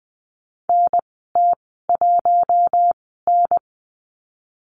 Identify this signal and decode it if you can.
Morse code, heard as a keyed tone, about 20 wpm, spaced as for 14 wpm, 700 Hz. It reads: NT1N
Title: 055 Lesson 10g - Letter r - Call Signs 20wpm